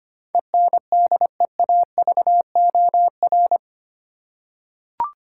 Morse code, heard as a keyed tone, about 25 wpm, 700 Hz. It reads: ENDEAVOR
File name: 8 Letter Words - First 3 Letters Given - 25wpm